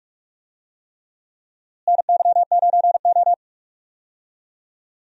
Morse code, heard as a keyed tone, about 45 wpm, 700 Hz. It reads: NY9O
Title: USA Call Signs - 4 Characters - Repeated 3x - 45wpm